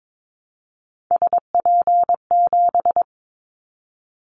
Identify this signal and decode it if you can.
Morse code, heard as a keyed tone, about 22 wpm, 700 Hz. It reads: SP7
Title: Callsign Prefixes for Top 25 DXCC - Repeated 3x - 22wpm